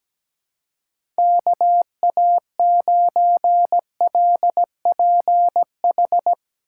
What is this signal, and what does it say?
Morse code, heard as a keyed tone, about 17 wpm, 700 Hz. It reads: KA9LPH
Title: USA Call Signs - 6 Characters - 2x Character Spacing - 17wpm